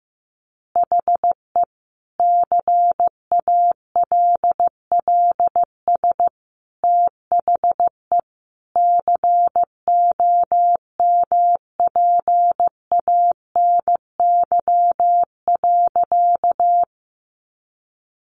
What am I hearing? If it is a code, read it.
Morse code, heard as a keyed tone, about 15 wpm, 700 Hz. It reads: HE CALLS THE COMPANY.
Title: Additional Easy Phrases 15wpm